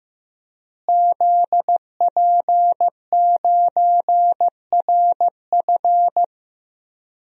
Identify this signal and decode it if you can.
Morse code, heard as a keyed tone, about 16 wpm, 700 Hz. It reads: ZP9RF
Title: DXCC-100 Call Signs - 5 Characters - Repeated 3x - 15wpm